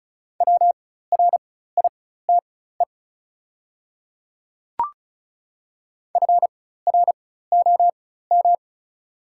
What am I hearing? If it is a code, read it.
Morse code, heard as a keyed tone, about 36 wpm, spaced as for 16 wpm, 700 Hz. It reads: WRITE FROM
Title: 4x Character Spacing - Top 100 Words - 35wpm